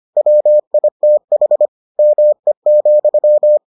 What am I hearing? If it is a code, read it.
Morse code, heard as a keyed tone, about 25 wpm, 600 Hz. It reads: WITH ME,